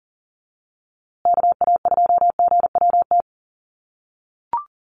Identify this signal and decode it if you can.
Morse code, heard as a keyed tone, about 40 wpm, 700 Hz. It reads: KA2GWT
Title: USA Call Signs Repeated 2x 40wpm